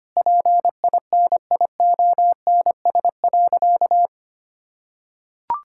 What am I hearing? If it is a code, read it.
Morse code, heard as a keyed tone, about 25 wpm, 700 Hz. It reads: PINIONS.
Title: Pangram Challenge v2 - 25wpm